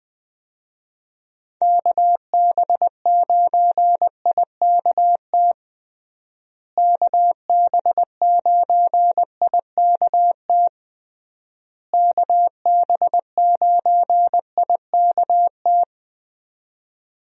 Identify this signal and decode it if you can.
Morse code, heard as a keyed tone, about 20 wpm, 700 Hz. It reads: KB9IKT KB9IKT KB9IKT
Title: USA Call Signs - 6 Characters - Repeated 3x - 20wpm